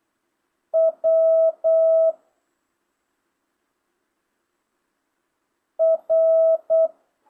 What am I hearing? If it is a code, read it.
Morse code, heard as a keyed tone, about 8 wpm, 650 Hz. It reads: W R
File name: jh8cbh / 2025cw